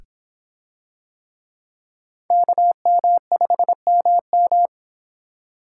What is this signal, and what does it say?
Morse code, heard as a keyed tone, about 22 wpm, 700 Hz. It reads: KM5MM